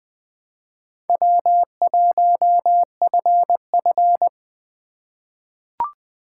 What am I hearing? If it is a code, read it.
Morse code, heard as a keyed tone, about 20 wpm, 700 Hz. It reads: W1FF
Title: USA Call Signs - 4 Characters - 4x Character Spacing - 20wpm